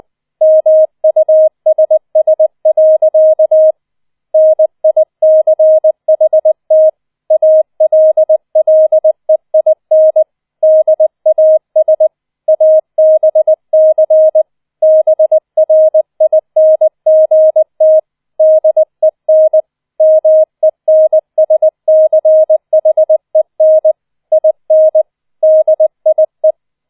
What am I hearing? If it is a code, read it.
Morse code, heard as a keyed tone, about 20 wpm, 630 Hz. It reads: MUSS. NICHT ALLEIN DAS ABC BRINGT DEN MENSCHEN IN DIE
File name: 004 Vierter Streich 20wpm